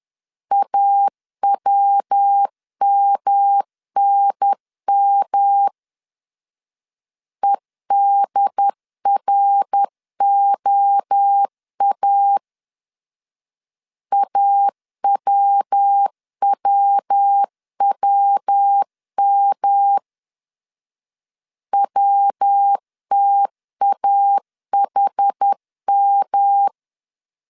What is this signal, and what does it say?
Morse code, heard as a keyed tone, about 10 wpm, 800 Hz. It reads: AWMNM EDROA AWWWM WTAHM
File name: CW / DET / 1 3 1500C S C 10wpm 3335